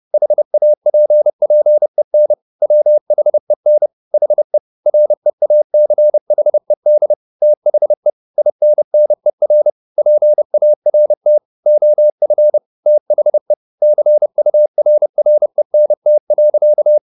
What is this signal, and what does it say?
Morse code, heard as a keyed tone, about 30 wpm, 600 Hz. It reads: HAPPEN WHEN HE REACHED THE INNER PART OF THE CURRENT.